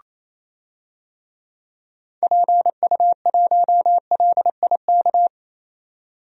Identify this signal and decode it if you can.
Morse code, heard as a keyed tone, about 27 wpm, 700 Hz. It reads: PU1LIK
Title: DXCC-100 Call Signs - 6 Characters - 28wpm